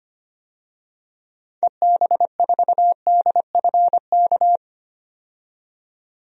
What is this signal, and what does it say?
Morse code, heard as a keyed tone, about 25 wpm, 700 Hz. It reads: EB4DFK
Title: DXCC-100 Call Signs - 6 Characters - 25wpm